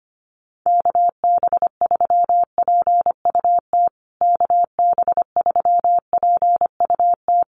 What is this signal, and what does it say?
Morse code, heard as a keyed tone, about 25 wpm, 700 Hz. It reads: KB3PUT KB3PUT
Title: USA Call Signs Repeated 3x 25wpm